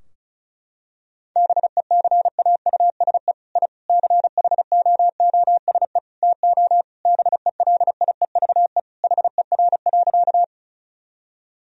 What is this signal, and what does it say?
Morse code, heard as a keyed tone, about 36 wpm, 700 Hz. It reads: BECAUSE I CHOOSE TO BELIEVE HER.
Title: Sentences from Top 1000 Words - Review Part 4 - 35wpm